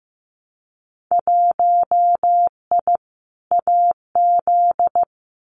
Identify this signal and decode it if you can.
Morse code, heard as a keyed tone, about 15 wpm, 700 Hz. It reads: 1I AZ
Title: Winter Field Day Exchanges 15wpm